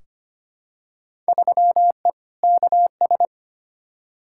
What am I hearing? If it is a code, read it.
Morse code, heard as a keyed tone, about 25 wpm, 700 Hz. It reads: 3E KS
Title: Field Day Exchanges Encore - 25wpm